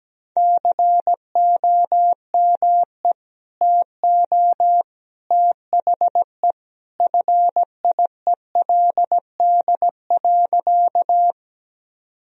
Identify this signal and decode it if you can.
Morse code, heard as a keyed tone, about 17 wpm, 700 Hz. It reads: COME TO THE FIELD.
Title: Sentences from Top 1000 Words - Review Part 2 - 17wpm